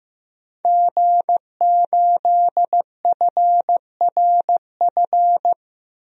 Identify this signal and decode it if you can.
Morse code, heard as a keyed tone, about 15 wpm, 700 Hz. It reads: G8FRF
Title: DXCC-100 Call Signs - 5 Characters - 15wpm